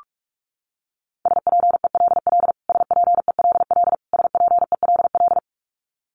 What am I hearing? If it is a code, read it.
Morse code, heard as a keyed tone, about 44 wpm, 700 Hz. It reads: SPELL SPELL SPELL
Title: Top 500 Words Repeated 45wpm